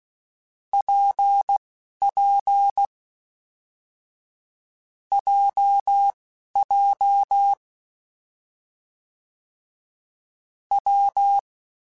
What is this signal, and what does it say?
Morse code, heard as a keyed tone, about 16 wpm, spaced as for 11 wpm, 800 Hz. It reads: PP JJ W
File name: Difficult Combos / Difficult Combos-mp3s / WPJ WPJ1 @10